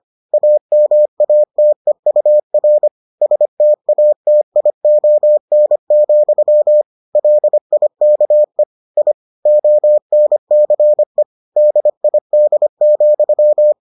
Morse code, heard as a keyed tone, about 25 wpm, 600 Hz. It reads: AMATEUR STATION, LIKE I ONCE DID,